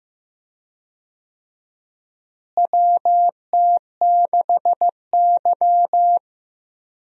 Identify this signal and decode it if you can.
Morse code, heard as a keyed tone, about 15 wpm, 700 Hz. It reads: WT6Y
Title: USA Call Signs - 4 Characters - Repeated 2x - 15wpm